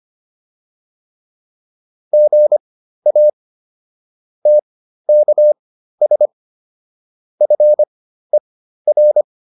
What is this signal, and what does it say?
Morse code, heard as a keyed tone, about 25 wpm, spaced as for 13 wpm, 600 Hz. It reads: GA TKS FER